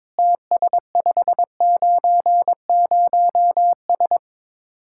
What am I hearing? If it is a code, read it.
Morse code, heard as a keyed tone, about 22 wpm, 700 Hz. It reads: TS590S
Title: Speed-Racing Common QSO Phrases Encore - 22wpm